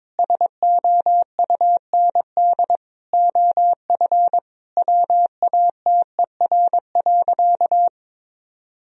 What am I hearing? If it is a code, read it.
Morse code, heard as a keyed tone, about 22 wpm, 700 Hz. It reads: SOUND OF WATER.
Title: Sentences from Top 1000 Words - Review Part 1 - 22wpm